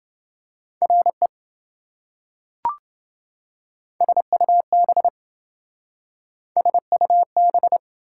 Morse code, heard as a keyed tone, about 27 wpm, 700 Hz. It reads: RE SUB SUB